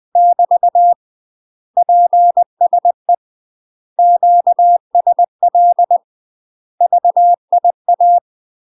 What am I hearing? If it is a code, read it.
Morse code, heard as a keyed tone, about 20 wpm, 700 Hz. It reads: <BT> PSE QSL VIA
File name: UK MASTER QSOs TRAINING / 20WPM / UK MASTER QSOs 0064